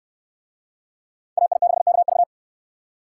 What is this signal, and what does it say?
Morse code, heard as a keyed tone, about 64 wpm, 700 Hz. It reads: AI6KV